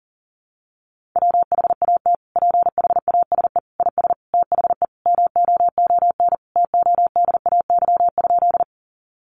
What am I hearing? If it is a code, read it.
Morse code, heard as a keyed tone, about 40 wpm, 700 Hz. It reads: WHAT PHASE IS THE MOON TODAY?
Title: Sentences from Top 1200 Words 40wpm